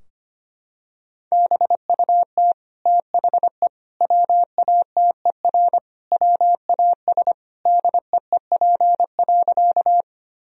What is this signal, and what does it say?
Morse code, heard as a keyed tone, about 25 wpm, 700 Hz. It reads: BUT THE WATER WAS DEEP.
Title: Sentences from Top 1000 Words - Review Part 3 - 25wpm